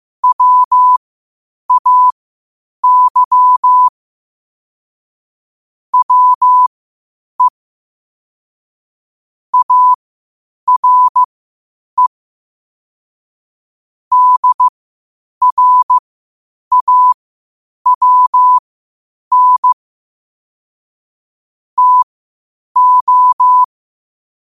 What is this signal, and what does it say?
Morse code, heard as a keyed tone, about 15 wpm, spaced as for 8 wpm, 1000 Hz. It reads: WAY WE ARE DRAWN TO
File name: qotdfiles / 8wpm